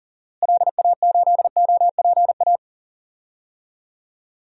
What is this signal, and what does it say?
Morse code, heard as a keyed tone, about 40 wpm, 700 Hz. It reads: LA8OPA
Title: DXCC-100 Call Signs - 6 Characters - Repeated 2x - 40wpm